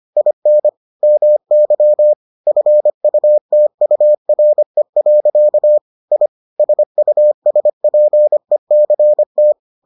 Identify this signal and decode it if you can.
Morse code, heard as a keyed tone, about 25 wpm, 600 Hz. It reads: IN MY FUTURE. I SUSPECT